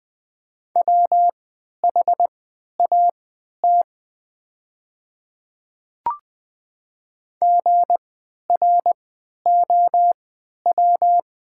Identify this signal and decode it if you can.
Morse code, heard as a keyed tone, about 20 wpm, spaced as for 11 wpm, 700 Hz. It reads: WHAT GROW
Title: 3x Character Spacing - Top 300 Words - 20wpm